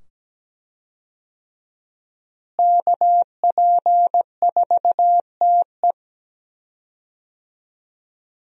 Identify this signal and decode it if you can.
Morse code, heard as a keyed tone, about 17 wpm, 700 Hz. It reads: KP4TE
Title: DXCC-100 Call Signs - 5 Characters - Repeated 2x - 17wpm